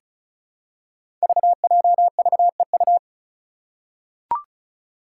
Morse code, heard as a keyed tone, about 36 wpm, 700 Hz. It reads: VJVEU
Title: Five Challenging Letter-Number Combinations 35wpm